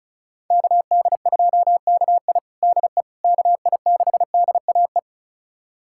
Morse code, heard as a keyed tone, about 34 wpm, 700 Hz. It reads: KD2KI DE KI6DAE
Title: USA Callsign DE Callsign 35wpm